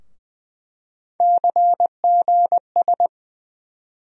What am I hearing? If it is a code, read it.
Morse code, heard as a keyed tone, about 21 wpm, 700 Hz. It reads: CGS
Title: USA Call Sign Postfixes - 3 Letters 20wpm